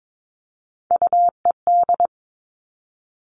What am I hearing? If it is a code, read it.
Morse code, heard as a keyed tone, about 22 wpm, 700 Hz. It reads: UED
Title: All Cut Numbers - 3-Digit 22wpm